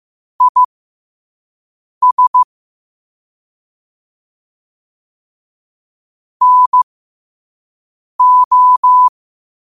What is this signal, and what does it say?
Morse code, heard as a keyed tone, about 15 wpm, spaced as for 5 wpm, 1000 Hz. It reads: IS NO
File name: qotdfiles / 5wpm